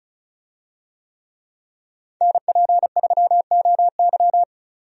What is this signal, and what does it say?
Morse code, heard as a keyed tone, about 36 wpm, 700 Hz. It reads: NP3OY
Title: USA Call Signs - 5 Characters - 35wpm